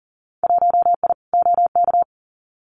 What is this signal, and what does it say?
Morse code, heard as a keyed tone, about 43 wpm, 700 Hz. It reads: 1I OK
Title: Winter Field Day Exchanges Encore 40wpm